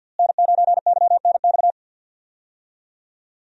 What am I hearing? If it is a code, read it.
Morse code, heard as a keyed tone, about 47 wpm, 690 Hz. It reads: N9YNX
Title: USA Call Signs - 5 Characters - Repeated 3x - 50wpm